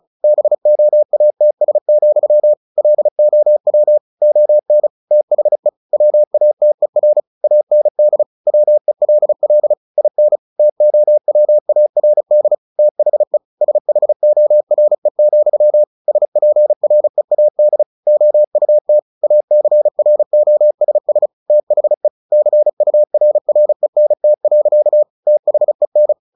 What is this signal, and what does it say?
Morse code, heard as a keyed tone, about 34 wpm, 600 Hz. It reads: BOATS, LOW ON THE WATER AND WELL IN TOWARD THE SHORE, SPREAD OUT ACROSS THE CURRENT. THEN
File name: the old man and the sea / the old man and the sea 1 35wpm